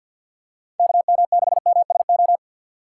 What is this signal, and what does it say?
Morse code, heard as a keyed tone, about 49 wpm, 700 Hz. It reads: KM6MSO
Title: USA Call Signs - 6 Characters - 4x Character Spacing - 50wpm